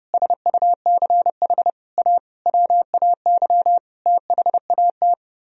Morse code, heard as a keyed tone, about 30 wpm, 700 Hz. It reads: SUCH A WAY THAT
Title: Speed-Racing - Sets of 5 Words 25wpm